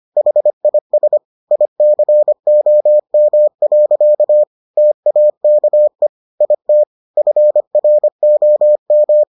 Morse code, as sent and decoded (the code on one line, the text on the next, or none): .... .. ... / .. -.-. --- -- .-.-.- / - .- -.- . / .. - / ..-. .-. --- --
HIS ICOM. TAKE IT FROM